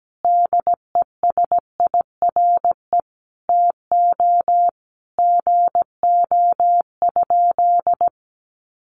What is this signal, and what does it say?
Morse code, heard as a keyed tone, about 17 wpm, 700 Hz. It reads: DESIRE TO GO?
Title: Sentences from Top 1100 Words 17wpm